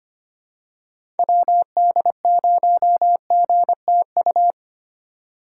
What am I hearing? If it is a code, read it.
Morse code, heard as a keyed tone, about 25 wpm, 700 Hz. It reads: WD0GTU
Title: USA Call Signs - 6 Characters - 4x Character Spacing - 25wpm